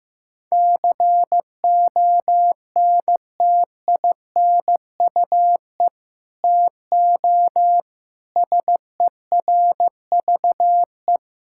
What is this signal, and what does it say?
Morse code, heard as a keyed tone, about 15 wpm, 700 Hz. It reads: CONTINUE TO SERVE
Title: Sentences from Top 1000 Words - Review Part 6 - 15wpm